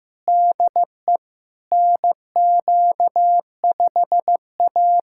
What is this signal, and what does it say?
Morse code, heard as a keyed tone, about 15 wpm, 700 Hz. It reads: DE NQ5A
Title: USA Callsign DE Callsign - 4 Characters - 15wpm